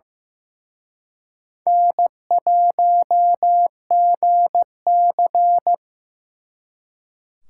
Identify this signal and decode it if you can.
Morse code, heard as a keyed tone, about 18 wpm, 700 Hz. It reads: N1GC
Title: Speed Racing - USA 1x2 Call Signs 15wpm